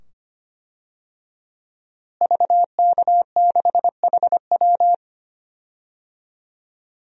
Morse code, heard as a keyed tone, about 25 wpm, 700 Hz. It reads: VK6HW
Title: DXCC-100 Call Signs - 5 Characters - Repeated 2x - 25wpm